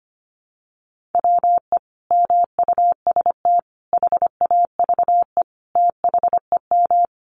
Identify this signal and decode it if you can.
Morse code, heard as a keyed tone, about 25 wpm, 700 Hz. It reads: WE MUST HAVE THEM
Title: Sentences from Top 1200 Words 25wpm